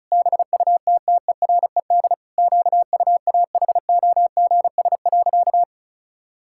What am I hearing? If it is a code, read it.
Morse code, heard as a keyed tone, about 34 wpm, 700 Hz. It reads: BUTTERED QUAHOGS.
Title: Pangram Challenge v2 - 35wpm